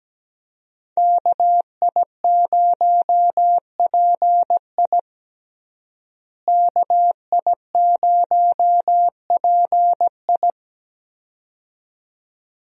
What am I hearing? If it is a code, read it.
Morse code, heard as a keyed tone, about 17 wpm, 700 Hz. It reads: KI0PI KI0PI
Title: USA Call Signs - 5 Characters - Repeated 2x - 17wpm